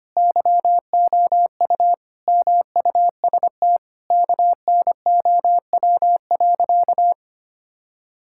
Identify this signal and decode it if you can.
Morse code, heard as a keyed tone, about 25 wpm, 700 Hz. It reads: YOU MUST KNOW.
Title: Sentences from Top 1000 Words - Review Part 6 - 25wpm